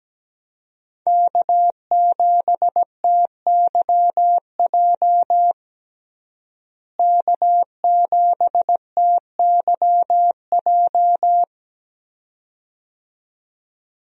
Morse code, heard as a keyed tone, about 17 wpm, 700 Hz. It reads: K7TYJ K7TYJ
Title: USA Call Signs - 5 Characters - Repeated 2x - 17wpm